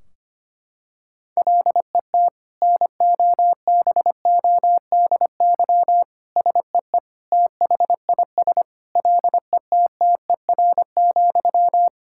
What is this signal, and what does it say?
Morse code, heard as a keyed tone, about 25 wpm, 700 Hz. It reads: LET NOBODY SEE THIS LETTER,
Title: Sentences from Top 1600 Words 25wpm